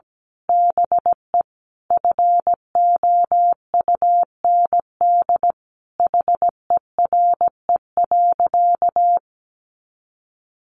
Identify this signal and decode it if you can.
Morse code, heard as a keyed tone, about 17 wpm, 700 Hz. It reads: BE FOUND HERE.
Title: Sentences from Top 1500 Words 17wpm